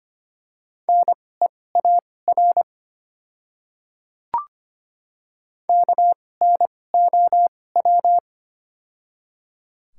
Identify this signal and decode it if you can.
Morse code, heard as a keyed tone, about 25 wpm, spaced as for 18 wpm, 700 Hz. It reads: NEAR KNOW
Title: 2x Character Spacing - Top 200 Words - 25wpm